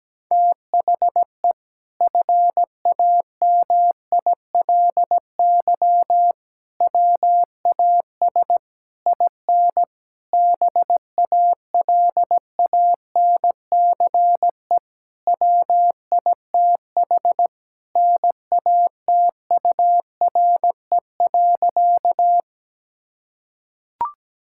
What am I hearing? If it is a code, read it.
Morse code, heard as a keyed tone, about 17 wpm, 700 Hz. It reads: THE FAMILY WAS IN BALANCE WITH NATURE.
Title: Sentences from Top 1000 Words - Review Part 6 - 17wpm